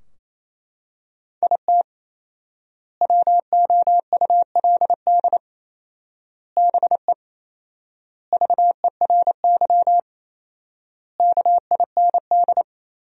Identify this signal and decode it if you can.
Morse code, heard as a keyed tone, about 28 wpm, 700 Hz. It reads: IT WOULD BE VERY KIND
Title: 4x Word-Spacing - Sentences from Top 200 Words - 28wpm